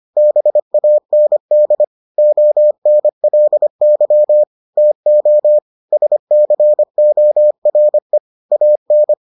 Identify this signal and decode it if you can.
Morse code, heard as a keyed tone, about 25 wpm, 600 Hz. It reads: BAND ONLY TO SCORE AN